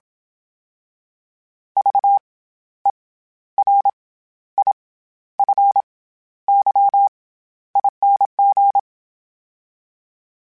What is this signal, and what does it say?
Morse code, heard as a keyed tone, about 27 wpm, 800 Hz. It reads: V E R I F Y ING
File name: Special INGs-mp3s / Special INGs-mp3s / SlowToFaster ing @25